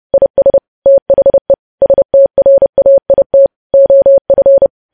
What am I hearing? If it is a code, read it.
Morse code, heard as a keyed tone, about 30 wpm, 550 Hz. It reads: IS THE STRAIT OF
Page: news headlines in morse code at 30 wpm